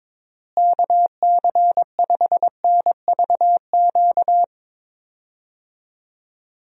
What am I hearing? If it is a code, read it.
Morse code, heard as a keyed tone, about 21 wpm, 700 Hz. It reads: KC5NVQ
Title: USA Call Signs - 6 Characters - Repeated 2x - 22wpm